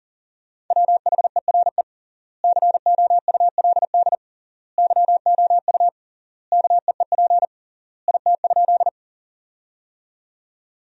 Morse code, heard as a keyed tone, about 37 wpm, 700 Hz. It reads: WHERE COULD YOU KEEP IT?
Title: 3x Word-Spacing - Sentences from Top 300 Words - 40wpm